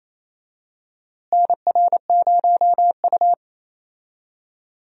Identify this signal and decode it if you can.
Morse code, heard as a keyed tone, about 28 wpm, 700 Hz. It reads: NR0U
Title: USA Call Signs - 4 Characters - Repeated 3x - 28wpm